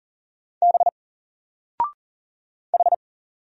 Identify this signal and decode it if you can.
Morse code, heard as a keyed tone, about 39 wpm, 700 Hz. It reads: B H
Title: Single Letter-Number 40wpm